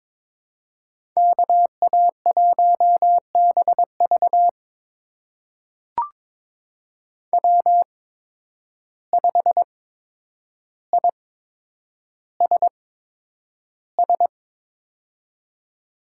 Morse code, heard as a keyed tone, about 22 wpm, spaced as for 9 wpm, 700 Hz. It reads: KA1BV W5ISS
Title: USA Call Signs - 8x Character Spacing - 22wpm